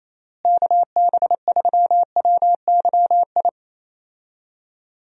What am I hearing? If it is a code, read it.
Morse code, heard as a keyed tone, about 28 wpm, 700 Hz. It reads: KB3WYI
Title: USA Call Signs - 6 Characters - Repeated 3x - 28wpm